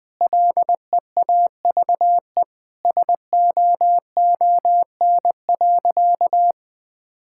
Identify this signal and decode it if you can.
Morse code, heard as a keyed tone, about 20 wpm, 700 Hz. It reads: LEAVE SOON.
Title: Sentences from Top 1000 Words - Review Part 5 - 20wpm